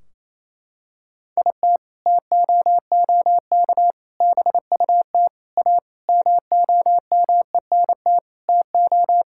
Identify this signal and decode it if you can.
Morse code, heard as a keyed tone, about 28 wpm, 700 Hz. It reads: IT TOOK BUT A MOMENT TO
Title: Sentences from Top 1600 Words 28wpm